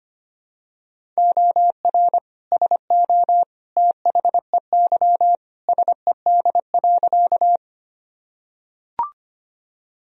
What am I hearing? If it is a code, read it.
Morse code, heard as a keyed tone, about 25 wpm, 700 Hz. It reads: OR SO THEY SED.
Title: Sentences from Top 100 Words and Abbreviations 25wpm